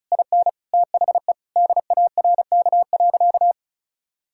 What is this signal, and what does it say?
Morse code, heard as a keyed tone, about 36 wpm, 700 Hz. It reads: IN THE DARK.
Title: Sentences from Top 1600 Words 35wpm